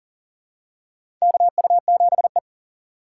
Morse code, heard as a keyed tone, about 40 wpm, 690 Hz. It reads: KU7E
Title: USA Call Signs - 4x Character Spacing - 40wpm